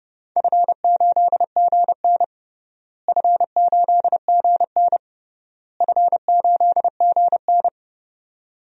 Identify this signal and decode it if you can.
Morse code, heard as a keyed tone, about 30 wpm, 700 Hz. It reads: F8GN F8GN F8GN
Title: DXCC-100 Call Signs - 4 Characters - Repeated 3x - 30wpm